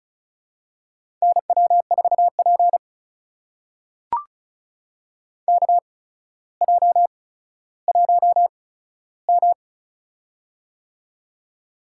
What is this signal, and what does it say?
Morse code, heard as a keyed tone, about 36 wpm, spaced as for 15 wpm, 700 Hz. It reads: NW4P KJ1M
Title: USA Call Signs - 4 Characters - 8x Character Spacing - 35wpm